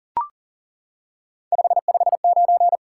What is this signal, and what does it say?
Morse code, heard as a keyed tone, about 39 wpm, 700 Hz. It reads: 559
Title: Top 90 QSO Elements 40wpm